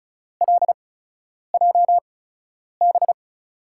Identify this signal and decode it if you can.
Morse code, heard as a keyed tone, about 35 wpm, spaced as for 10 wpm, 700 Hz. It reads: LJB
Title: USA Call Signs - 6 Characters - 8x Character Spacing - 35wpm